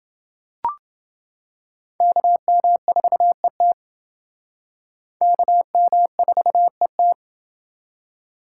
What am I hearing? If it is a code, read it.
Morse code, heard as a keyed tone, about 28 wpm, 700 Hz. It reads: KM4ET KM4ET